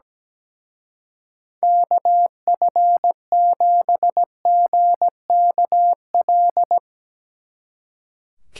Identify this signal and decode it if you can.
Morse code, heard as a keyed tone, about 17 wpm, 700 Hz. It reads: KF7GKL
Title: Speed Racing - USA 2x3 Call Signs 17wpm